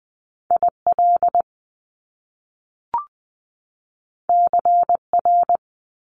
Morse code, heard as a keyed tone, about 20 wpm, 700 Hz. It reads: IL CR
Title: Top Two Letter Combinations in Words 20wpm